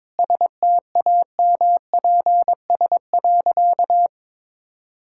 Morse code, heard as a keyed tone, about 22 wpm, 700 Hz. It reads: STAMPS.
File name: practice / 22wpm / Easy Sentences with Hints - 22wpm